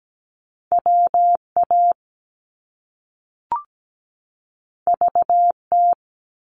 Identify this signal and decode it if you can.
Morse code, heard as a keyed tone, about 17 wpm, 700 Hz. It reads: WA VT
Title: US States 17wpm